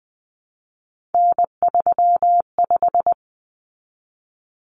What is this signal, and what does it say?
Morse code, heard as a keyed tone, about 20 wpm, 700 Hz. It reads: N35
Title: Common Cut Numbers - 3-Digit 20wpm